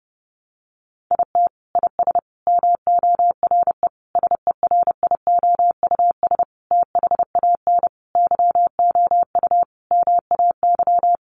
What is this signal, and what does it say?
Morse code, heard as a keyed tone, about 30 wpm, 700 Hz. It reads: IT IS MORE SERIOUS THAN YOU MAY
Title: Sentences from Top 1100 Words 30wpm